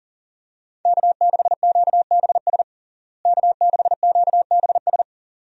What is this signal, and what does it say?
Morse code, heard as a keyed tone, about 40 wpm, 700 Hz. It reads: K6QBS K6QBS
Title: USA Call Signs - 5 Characters - Repeated 2x - 40wpm